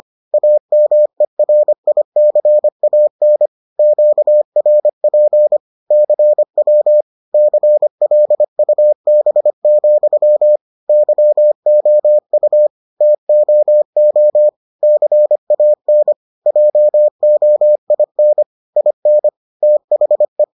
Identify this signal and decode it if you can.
Morse code, heard as a keyed tone, about 25 wpm, 600 Hz. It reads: AMERICAN QRP CW CLUB, YOU TOO CAN JOIN IN THE